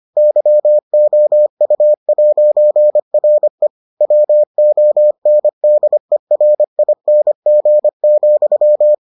Morse code, heard as a keyed tone, about 25 wpm, 600 Hz. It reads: YOU'RE WONDERING,